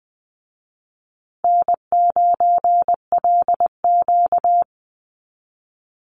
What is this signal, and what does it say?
Morse code, heard as a keyed tone, about 20 wpm, 700 Hz. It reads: N9LQ
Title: Easy Call Signs Repeated 2x - 20wpm